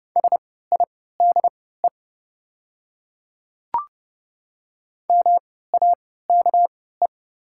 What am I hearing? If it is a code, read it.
Morse code, heard as a keyed tone, about 30 wpm, spaced as for 17 wpm, 700 Hz. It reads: SIDE MAKE
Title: 3x Character Spacing - Top 200 Words - 30wpm